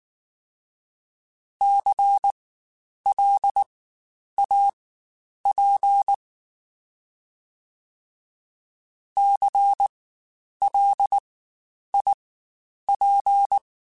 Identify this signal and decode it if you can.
Morse code, heard as a keyed tone, about 19 wpm, spaced as for 9 wpm, 800 Hz. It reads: CLAP CLIP
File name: Spaced Alphas / Spaced Alphas mp3s / Cs sp @18